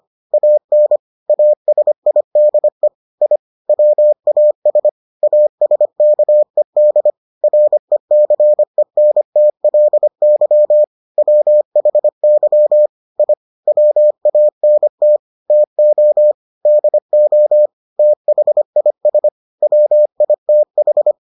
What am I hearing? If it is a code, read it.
Morse code, heard as a keyed tone, about 25 wpm, 600 Hz. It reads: AN ASIDE I WAS ASKED RECENTLY WHY I WANT TO DO THIS WITH